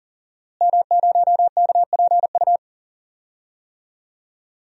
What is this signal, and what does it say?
Morse code, heard as a keyed tone, about 40 wpm, 700 Hz. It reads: M0KPU
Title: DXCC-100 Call Signs - 5 Characters - Repeated 3x - 40wpm